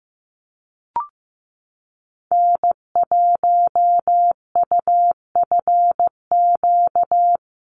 Morse code, heard as a keyed tone, about 15 wpm, 700 Hz. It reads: N1UFQ
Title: USA Call Signs Repeated 3x 15wpm